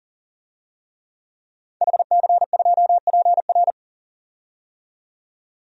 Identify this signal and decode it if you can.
Morse code, heard as a keyed tone, about 40 wpm, 700 Hz. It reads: HC2PR
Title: DXCC-100 Call Signs - 5 Characters - 40wpm